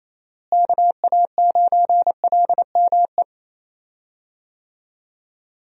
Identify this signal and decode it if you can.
Morse code, heard as a keyed tone, about 28 wpm, 700 Hz. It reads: KA9LME
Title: USA Call Signs - 6 Characters - Repeated 3x - 28wpm